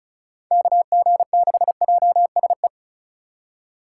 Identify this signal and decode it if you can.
Morse code, heard as a keyed tone, about 36 wpm, 690 Hz. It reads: KG6JSE
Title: USA Call Signs - 6 Characters - 4x Character Spacing - 35wpm